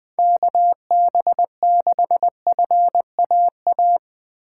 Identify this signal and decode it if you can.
Morse code, heard as a keyed tone, about 20 wpm, 700 Hz. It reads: KB6FAA
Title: USA Call Signs - 2x Character Spacing - 20wpm